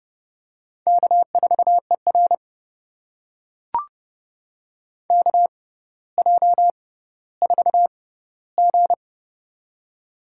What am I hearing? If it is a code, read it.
Morse code, heard as a keyed tone, about 30 wpm, spaced as for 15 wpm, 700 Hz. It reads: K4ER KJ4G
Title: USA Call Signs - 4 Characters - 6x Character Spacing - 30wpm